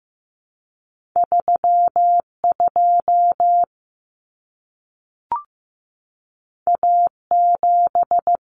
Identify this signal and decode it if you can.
Morse code, heard as a keyed tone, about 15 wpm, 700 Hz. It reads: 32 A7
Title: Common Cut Numbers - 2-Digit 15wpm